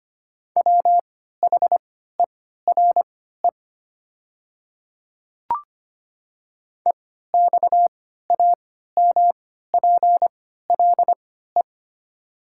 Morse code, heard as a keyed tone, about 25 wpm, spaced as for 14 wpm, 700 Hz. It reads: WHERE EXAMPLE
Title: 3x Character Spacing - Top 300 Words - 25wpm